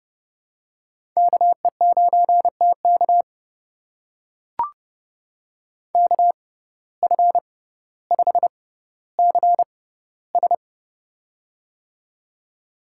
Morse code, heard as a keyed tone, about 30 wpm, spaced as for 15 wpm, 700 Hz. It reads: KE9TK KF5CS
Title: USA Call Signs - 5 Characters - 6x Character Spacing - 30wpm